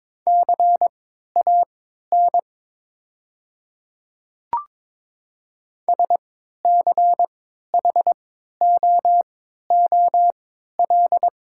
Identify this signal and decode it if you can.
Morse code, heard as a keyed tone, about 22 wpm, spaced as for 12 wpm, 690 Hz. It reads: CAN SCHOOL
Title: 3x Character Spacing - Top 500 Words - 22wpm